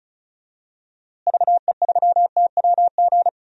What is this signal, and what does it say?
Morse code, heard as a keyed tone, about 34 wpm, 700 Hz. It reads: VE3TWG
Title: DXCC-100 Call Signs - 6 Characters - Repeated 3x - 35wpm